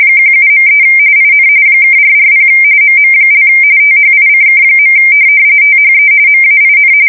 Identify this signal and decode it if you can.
RTTY, about 45.45 baud. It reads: 17.3 IN (440 MM) THICK. THE MAIN
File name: fsk170 / USS-Missouri